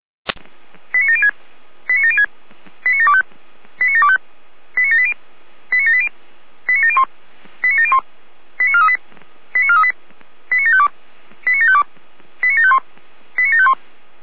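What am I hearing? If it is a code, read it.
Five-tone selective call (ZVEI), heard as quick runs of five tones, tones of about 80 ms. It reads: 89706 89706 89725 89725 89780 89780 89701 89701 89438 89438 89752 89752 89751 89751
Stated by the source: ZVEI five-tone selective call received on an SDR.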